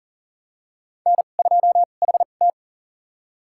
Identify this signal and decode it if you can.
Morse code, heard as a keyed tone, about 37 wpm, spaced as for 28 wpm, 690 Hz. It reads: N2HT
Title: USA Call Signs - 4 Characters - 2x Character Spacing - 40wpm